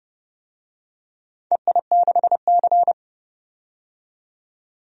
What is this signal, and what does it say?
Morse code, heard as a keyed tone, about 31 wpm, 700 Hz. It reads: EI6C
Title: DXCC-100 Call Signs - 4 Characters - 30wpm